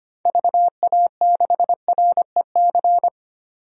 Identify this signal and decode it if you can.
Morse code, heard as a keyed tone, about 25 wpm, 700 Hz. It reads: VA6REC
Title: DXCC-100 Call Signs - 6 Characters - 25wpm